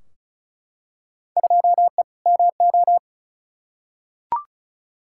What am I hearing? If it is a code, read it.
Morse code, heard as a keyed tone, about 36 wpm, 700 Hz. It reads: 2E MO
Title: Field Day Exchanges - 35wpm